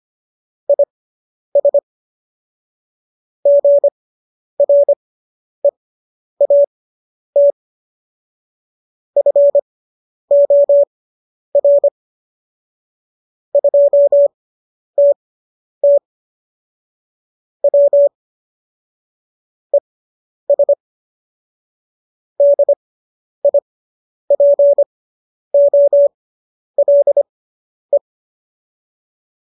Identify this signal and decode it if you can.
Morse code, heard as a keyed tone, about 25 wpm, spaced as for 10 wpm, 570 Hz. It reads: IS GREAT FOR 2TT W ES DIPOLE